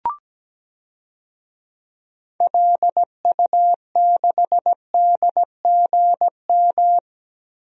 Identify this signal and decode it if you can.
Morse code, heard as a keyed tone, about 17 wpm, 700 Hz. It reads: LU6DGM
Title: DXCC-100 Call Signs - 6 Characters - 17wpm